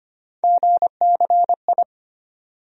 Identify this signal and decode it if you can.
Morse code, heard as a keyed tone, about 25 wpm, 700 Hz. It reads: GCI
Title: USA Call Sign Postfixes - 3 Letters 25wpm